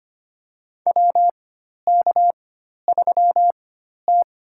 Morse code, heard as a keyed tone, about 25 wpm, spaced as for 12 wpm, 700 Hz. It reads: WK3T
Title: USA Call Signs - 4 Characters - 4x Character Spacing - 25wpm